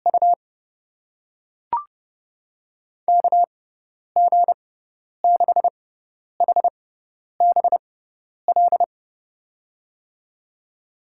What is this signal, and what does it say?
Morse code, heard as a keyed tone, about 32 wpm, spaced as for 10 wpm, 700 Hz. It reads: U KG6HBL
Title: USA Call Signs - 6 Characters - 6x Character Spacing - 30wpm